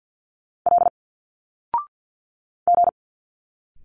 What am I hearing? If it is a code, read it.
Morse code, heard as a keyed tone, about 50 wpm, 700 Hz. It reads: L G
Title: Single Letters 50wpm